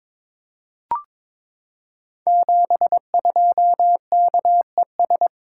Morse code, heard as a keyed tone, about 22 wpm, 700 Hz. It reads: 72KES
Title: Code Groups - 5 Letter:Numbers - 22wpm